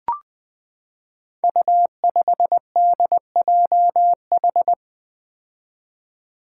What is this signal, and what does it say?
Morse code, heard as a keyed tone, about 20 wpm, 700 Hz. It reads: U5DJH
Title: Five Challenging Letter-Number Combinations 20wpm